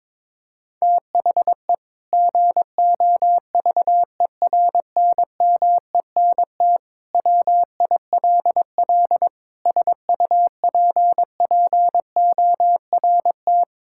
Morse code, heard as a keyed tone, about 22 wpm, 700 Hz. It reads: THE GOVERNMENT WILL SUPPORT
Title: Sentences from Top 1000 Words - Review Part 5 - 22wpm